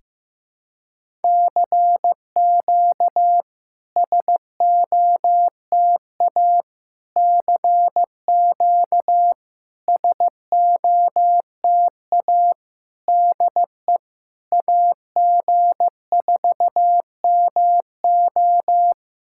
Morse code, heard as a keyed tone, about 14 wpm, 700 Hz. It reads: CQ SOTA CQ SOTA DE AG4MO
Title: SOTA QSO Practice 15wpm